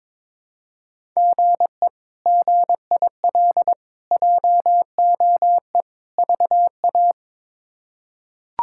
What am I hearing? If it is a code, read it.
Morse code, heard as a keyed tone, about 22 wpm, 700 Hz. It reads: GE GIL JOE VA
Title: K1USN SST QSO Practice - 22wpm